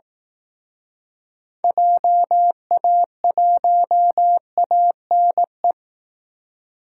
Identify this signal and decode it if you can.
Morse code, heard as a keyed tone, about 18 wpm, 700 Hz. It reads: JA1ANE
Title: Speed Racing - USA 2x3 Call Signs 15wpm